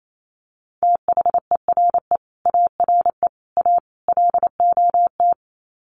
Morse code, heard as a keyed tone, about 28 wpm, 700 Hz. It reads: THERE ARE A LOT
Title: Sets of 4 Words 28wpm